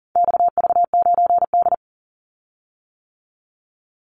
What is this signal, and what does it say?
Morse code, heard as a keyed tone, about 41 wpm, 700 Hz. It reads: XV9D
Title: Call Signs - 40wpm